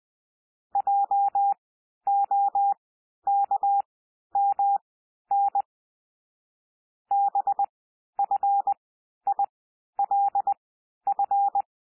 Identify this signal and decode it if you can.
Morse code, heard as a keyed tone, about 20 wpm, spaced as for 11 wpm, 800 Hz. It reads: JOKMN BFILF
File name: cwmp3 / L3 SOUND LETTERS 20 wpm